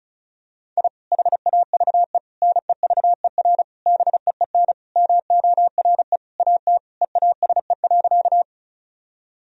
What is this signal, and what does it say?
Morse code, heard as a keyed tone, about 34 wpm, 700 Hz. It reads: I HAVE NEVER BEEN MORE AT EASE.
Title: Sentences from Top 1000 Words - Review Part 1 - 35wpm